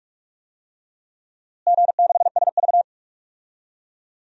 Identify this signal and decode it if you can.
Morse code, heard as a keyed tone, about 43 wpm, 700 Hz. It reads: G6SV
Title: DXCC-100 Call Signs - 4 Characters - Repeated 2x - 45wpm